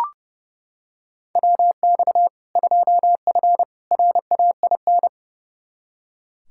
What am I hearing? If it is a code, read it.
Morse code, heard as a keyed tone, about 30 wpm, 700 Hz. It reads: WX 2F RAIN
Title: K1USN SST QSO Practice 30wpm